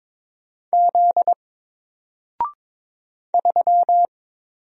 Morse code, heard as a keyed tone, about 22 wpm, 700 Hz. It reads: Z 3
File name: Single-Letter-Number-22wpm